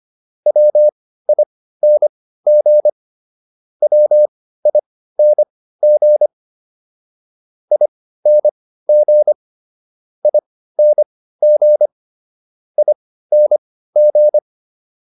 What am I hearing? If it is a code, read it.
Morse code, heard as a keyed tone, about 25 wpm, spaced as for 15 wpm, 600 Hz. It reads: WING WING ING ING ING